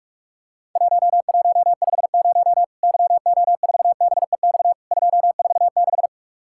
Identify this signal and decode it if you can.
Morse code, heard as a keyed tone, about 44 wpm, 700 Hz. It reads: 1150 YO4BEX 246
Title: Worked All Europe DX Contest - QTC Exchange - 45wpm